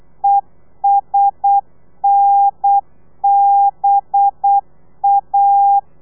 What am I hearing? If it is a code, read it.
Morse code, heard as a keyed tone, about 8 wpm, 800 Hz. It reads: ESNBA
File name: telegrafia / i2tfj / Lez4 35